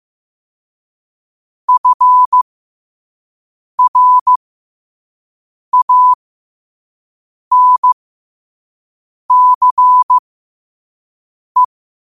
Morse code, heard as a keyed tone, about 15 wpm, spaced as for 5 wpm, 1000 Hz. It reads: FRANCE
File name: qotdfiles / 5wpm